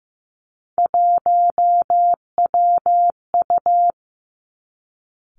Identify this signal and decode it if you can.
Morse code, heard as a keyed tone, about 15 wpm, 700 Hz. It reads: 1WU
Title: Three Challenging Letter-Number Combinations 15wpm